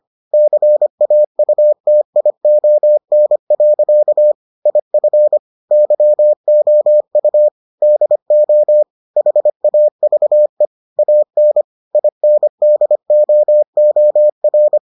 Morse code, heard as a keyed tone, about 25 wpm, 600 Hz. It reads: CAUTION. IF YOU DO HAVE AN INDOOR